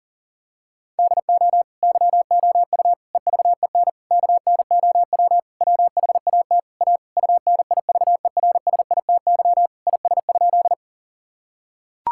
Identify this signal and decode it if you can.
Morse code, heard as a keyed tone, about 37 wpm, 700 Hz. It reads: DO YOU EVEN KNOW WHAT A UNIVERSITY IS?
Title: Sentences from Top 1000 Words - Review Part 5 - 40wpm